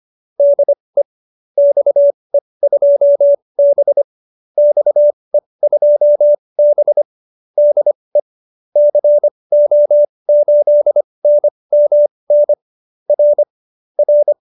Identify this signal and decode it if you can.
Morse code, heard as a keyed tone, about 25 wpm, spaced as for 20 wpm, 590 Hz. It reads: DE XE2B XE2B DE CO8NMN R R